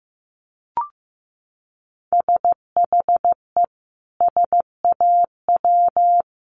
Morse code, heard as a keyed tone, about 15 wpm, 700 Hz. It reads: SHE SAW
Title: Sentences from Top 1200 Words 15wpm